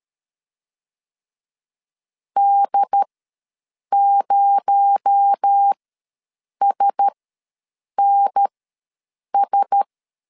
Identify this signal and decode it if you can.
Morse code, heard as a keyed tone, about 13 wpm, spaced as for 7 wpm, 800 Hz. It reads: D0SNS
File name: LEZIONI / 6 PKX0 / 6 PKX0 ws 12wpm 3448